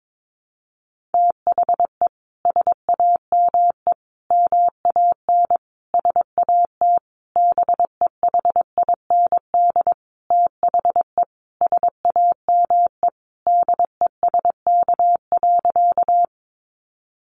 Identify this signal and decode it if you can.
Morse code, heard as a keyed tone, about 22 wpm, 700 Hz. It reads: THE SAME MAN SAT BEHIND THE SAME DESK.
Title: A Sound Of Thunder 22wpm